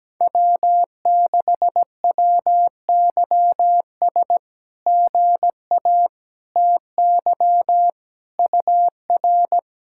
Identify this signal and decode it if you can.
Morse code, heard as a keyed tone, about 17 wpm, 700 Hz. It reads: W6WYS GA TY UR
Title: SOTA QSO Practice 17wpm